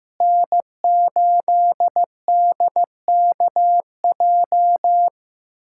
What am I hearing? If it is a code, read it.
Morse code, heard as a keyed tone, about 15 wpm, 700 Hz. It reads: N8DKJ
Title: USA Callsign DE Callsign 15wpm